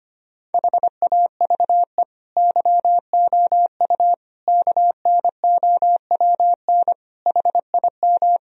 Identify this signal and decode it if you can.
Morse code, heard as a keyed tone, about 25 wpm, 700 Hz. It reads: HAVE YOU KNOWN HIM
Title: Sentences from Top 1000 Words - Review Part 3 - 25wpm